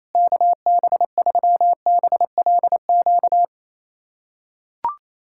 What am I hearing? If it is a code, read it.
Morse code, heard as a keyed tone, about 28 wpm, 700 Hz. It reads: KB3BLQ
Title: USA Callsign DE Callsign - 6 Characters - 28wpm